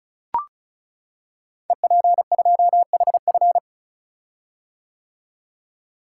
Morse code, heard as a keyed tone, about 36 wpm, 700 Hz. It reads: EP2HF
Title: Code Groups - 5 Letter:Numbers - 35wpm